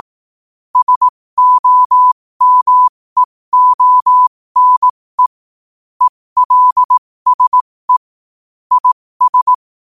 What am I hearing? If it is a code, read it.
Morse code, heard as a keyed tone, about 18 wpm, spaced as for 15 wpm, 1000 Hz. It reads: SOMEONE ELSE IS